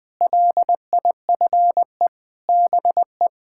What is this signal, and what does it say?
Morse code, heard as a keyed tone, about 20 wpm, 700 Hz. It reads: LIFE BE
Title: Sentences with 6-Word Phrases - 20wpm